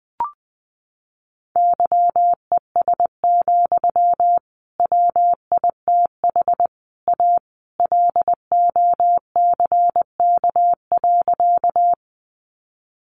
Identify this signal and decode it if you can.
Morse code, heard as a keyed tone, about 19 wpm, 700 Hz. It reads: YES, WITH A LOCK.
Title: Sentences from Top 1300 Words 20wpm